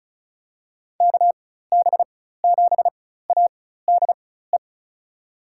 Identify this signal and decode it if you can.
Morse code, heard as a keyed tone, about 36 wpm, spaced as for 16 wpm, 700 Hz. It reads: KB7ADE
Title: USA Call Signs - 4x Character Spacing - 35wpm